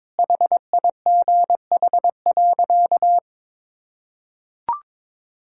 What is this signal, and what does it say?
Morse code, heard as a keyed tone, about 21 wpm, 700 Hz. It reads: HIGH.
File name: practice / 22wpm / Sentences with 3-Word Phrases - 22wpm